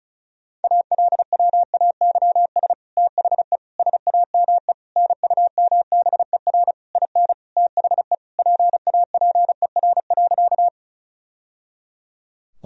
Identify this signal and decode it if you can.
Morse code, heard as a keyed tone, about 36 wpm, 700 Hz. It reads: ALWAYS THE SAME NUMBER IN THE PAPER.
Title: Sentences from Top 1000 Words - Review Part 1 - 35wpm